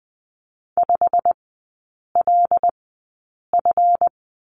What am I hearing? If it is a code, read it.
Morse code, heard as a keyed tone, about 20 wpm, spaced as for 8 wpm, 700 Hz. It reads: 5LF
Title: 138 Lesson 23e - Letter f - Three Characters 20wpm